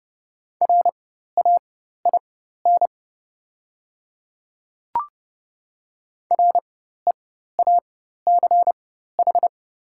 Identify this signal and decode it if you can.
Morse code, heard as a keyed tone, about 32 wpm, spaced as for 14 wpm, 690 Hz. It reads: RAIN REACH
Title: 4x Character Spacing - Top 500 Words - 30wpm